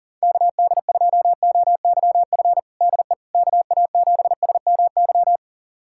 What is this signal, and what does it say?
Morse code, heard as a keyed tone, about 40 wpm, 690 Hz. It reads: KD2OYF DE KA7SMY
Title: USA Callsign DE Callsign - 6 Characters - 40wpm